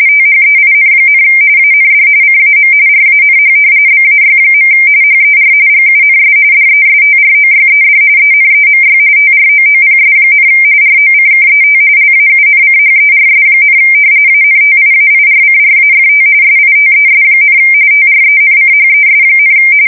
RTTY, about 45.45 baud. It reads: THEIN, AND CAM RANH BAY. HER GUNS DESTROYED ENEMY TROOPS, ROCKET LAUNCHERS, ANTIAIRCRAFT SITES, TRENCHES,
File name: fsk170 / battleshipnj